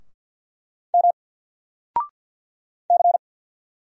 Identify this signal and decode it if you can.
Morse code, heard as a keyed tone, about 50 wpm, 700 Hz. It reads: M C
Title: Single Letter-Number 50wpm